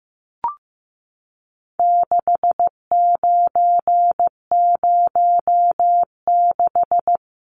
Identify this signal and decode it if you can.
Morse code, heard as a keyed tone, about 15 wpm, 700 Hz. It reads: 6906
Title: Numbers - 4-Digit 15wpm